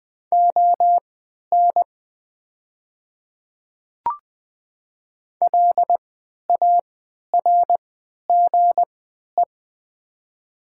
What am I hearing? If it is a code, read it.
Morse code, heard as a keyed tone, about 20 wpm, spaced as for 11 wpm, 690 Hz. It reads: ON LARGE
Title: 3x Character Spacing - Top 200 Words - 20wpm